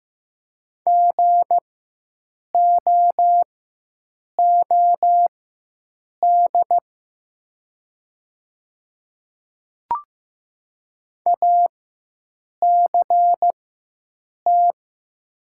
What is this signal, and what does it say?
Morse code, heard as a keyed tone, about 15 wpm, spaced as for 7 wpm, 700 Hz. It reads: GOOD ACT
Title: 4x Character Spacing - Top 200 Words - 15wpm